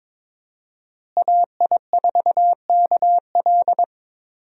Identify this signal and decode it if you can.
Morse code, heard as a keyed tone, about 22 wpm, 700 Hz. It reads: AI4KL
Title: USA Call Signs - 5 Characters - Repeated 2x - 22wpm